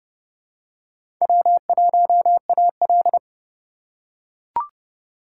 Morse code, heard as a keyed tone, about 29 wpm, 700 Hz. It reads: W1AL